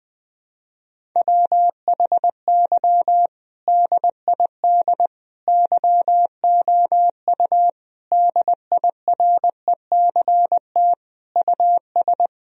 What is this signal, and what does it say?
Morse code, heard as a keyed tone, about 20 wpm, 700 Hz. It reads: WHY DID YOU DIRECT US
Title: Sentences from Top 1000 Words - Review Part 2 - 20wpm